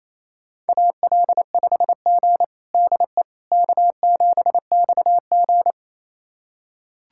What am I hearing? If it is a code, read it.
Morse code, heard as a keyed tone, about 28 wpm, 700 Hz. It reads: AL5G DE K7XG
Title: USA Callsign DE Callsign - 4 Characters - 28wpm